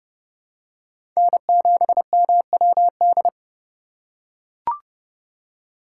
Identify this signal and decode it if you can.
Morse code, heard as a keyed tone, about 30 wpm, 700 Hz. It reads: N7MWD
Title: USA Call Signs - 5 Characters - 3x Character Spacing - 30wpm